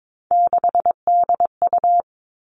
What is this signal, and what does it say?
Morse code, heard as a keyed tone, about 22 wpm, 700 Hz. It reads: 6DU
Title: All Cut Numbers - 3-Digit 22wpm